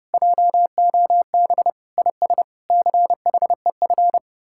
Morse code, heard as a keyed tone, about 40 wpm, spaced as for 35 wpm, 700 Hz. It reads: JOB IS CHEF
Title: Speed-Racing Common QSO Phrases - 30wpm